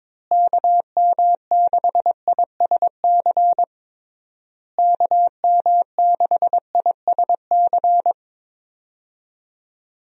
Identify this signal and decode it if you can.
Morse code, heard as a keyed tone, about 22 wpm, 700 Hz. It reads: KM6ISC KM6ISC
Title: USA Call Signs - 6 Characters - Repeated 3x - 22wpm